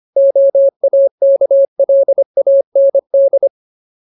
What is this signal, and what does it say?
Morse code, heard as a keyed tone, about 25 wpm, 550 Hz. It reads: OAKLAND
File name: Ultimate Word List / 25WPM Ultimate